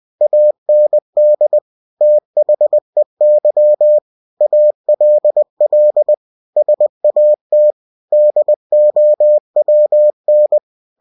Morse code, heard as a keyed tone, about 20 wpm, 600 Hz. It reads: AND THEY ALL SAT DOWN
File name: cw-books / winnie-the-pooh / winnie-20wpm-page-0083